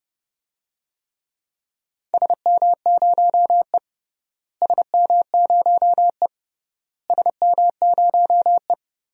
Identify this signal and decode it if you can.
Morse code, heard as a keyed tone, about 31 wpm, 700 Hz. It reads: SM0E SM0E SM0E
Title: DXCC-100 Call Signs - 4 Characters - Repeated 3x - 30wpm